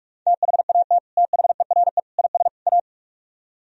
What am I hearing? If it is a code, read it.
Morse code, heard as a keyed tone, about 64 wpm, spaced as for 54 wpm, 700 Hz. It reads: THAT THERE IS A
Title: Speed-Racing - Sets of 4 Words 45wpm